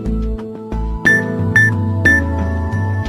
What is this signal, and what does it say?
There is no code.